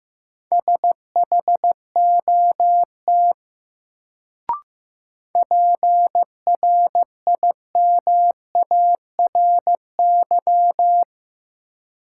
Morse code, heard as a keyed tone, about 15 wpm, 700 Hz. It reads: SHOT PRIMARY
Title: Sentences from Top 1100 Words - New Word Drill Alt - 15wpm